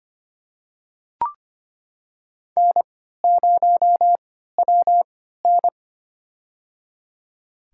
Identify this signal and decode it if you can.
Morse code, heard as a keyed tone, about 25 wpm, spaced as for 14 wpm, 700 Hz. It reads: N0WN
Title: USA Call Signs - 4 Characters - 3x Character Spacing - 25wpm